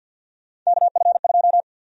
Morse code, heard as a keyed tone, about 53 wpm, 700 Hz. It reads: KF2
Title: USA Call Sign Prefixes - 50wpm